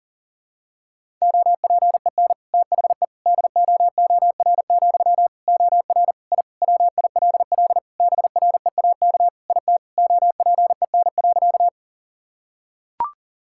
Morse code, heard as a keyed tone, about 40 wpm, 700 Hz. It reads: OPEN THE DOOR, OR I WILL BREAK IT OPEN.
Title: Sentences from Top 1000 Words - Review Part 6 - 40wpm